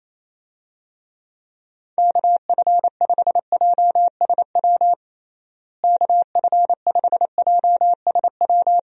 Morse code, heard as a keyed tone, about 27 wpm, 700 Hz. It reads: KF5JSW KF5JSW
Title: USA Call Signs - 6 Characters - Repeated 3x - 28wpm